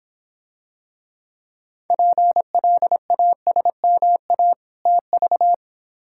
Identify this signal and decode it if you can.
Morse code, heard as a keyed tone, about 26 wpm, 700 Hz. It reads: PLASMA TV